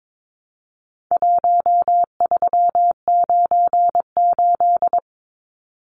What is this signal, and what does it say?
Morse code, heard as a keyed tone, about 22 wpm, 700 Hz. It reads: 1398
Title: Numbers - Contest Style - 4-Digit 22wpm